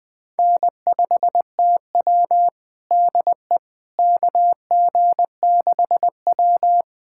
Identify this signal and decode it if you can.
Morse code, heard as a keyed tone, about 20 wpm, 700 Hz. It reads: N5TW DE KG6W
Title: USA Callsign DE Callsign - 4 Characters - 20wpm